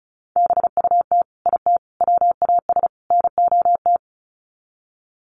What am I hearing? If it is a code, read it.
Morse code, heard as a keyed tone, about 34 wpm, 700 Hz. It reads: BUT IT WAS NOT
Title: Sets of 4 Words 35wpm